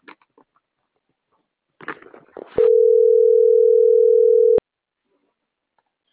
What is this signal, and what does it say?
Ringback tone (North America), 440 + 480 Hz.